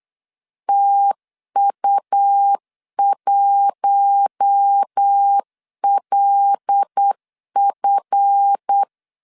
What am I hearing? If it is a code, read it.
Morse code, heard as a keyed tone, about 8 wpm, 800 Hz. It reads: TU1LF
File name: LEZIONI / 10 SIGNS / 10B INT 8wpm 3534